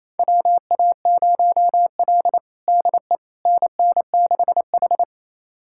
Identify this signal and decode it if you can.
Morse code, heard as a keyed tone, about 27 wpm, 700 Hz. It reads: WA0L DE NN6H
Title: USA Callsign DE Callsign - 4 Characters - 28wpm